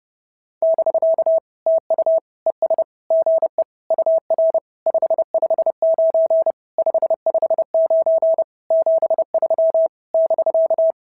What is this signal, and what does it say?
Morse code, heard as a keyed tone, about 30 wpm, 650 Hz. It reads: <BK> TU ES GE UR 559 559 73 <BK>
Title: SOTA QSO Practice 30wpm